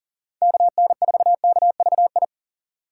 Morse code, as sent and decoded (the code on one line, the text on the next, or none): -.- -. ....- -.- ...- ..
KN4KVI